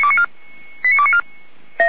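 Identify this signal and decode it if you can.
Five-tone selective call (ZVEI), heard as quick runs of five tones, tones of about 70 ms. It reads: 89294
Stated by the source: ZVEI five-tone selective call received with an SDR.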